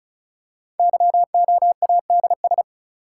Morse code, heard as a keyed tone, about 36 wpm, 700 Hz. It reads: YOADS
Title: Code Groups - 5 Letter:Numbers - 35wpm